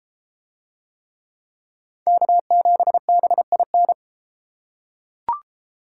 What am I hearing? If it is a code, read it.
Morse code, heard as a keyed tone, about 32 wpm, 700 Hz. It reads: K7BIN